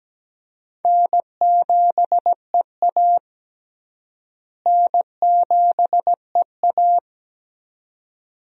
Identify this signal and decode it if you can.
Morse code, heard as a keyed tone, about 17 wpm, 700 Hz. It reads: N7EA N7EA
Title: USA Call Signs - 4 Characters - Repeated 2x - 17wpm